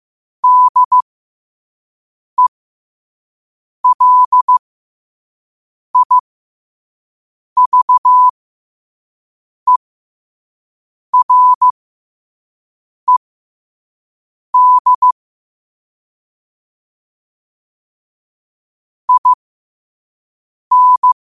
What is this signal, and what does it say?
Morse code, heard as a keyed tone, about 15 wpm, spaced as for 5 wpm, 1000 Hz. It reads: DELIVERED IN
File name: qotdfiles / 5wpm